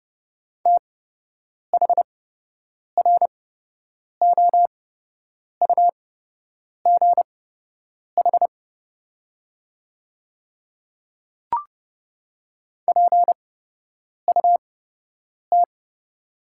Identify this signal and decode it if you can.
Morse code, heard as a keyed tone, about 30 wpm, spaced as for 8 wpm, 700 Hz. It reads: THROUGH PUT
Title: 8x Character Spacing - Top 500 Words - 30wpm